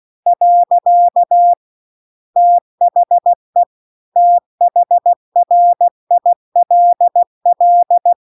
Morse code, heard as a keyed tone, about 16 wpm, 700 Hz. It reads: . THE THRILL
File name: CW-The-Big-Twoheaded-River-2-16WPM